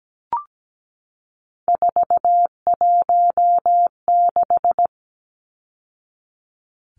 Morse code, heard as a keyed tone, about 17 wpm, 700 Hz. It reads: 416
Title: Numbers - 3-Digit 17wpm